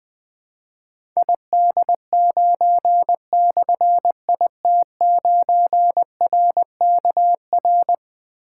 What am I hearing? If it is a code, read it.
Morse code, heard as a keyed tone, about 20 wpm, 700 Hz. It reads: ID9/IT9RKR
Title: Difficult Call Signs 20wpm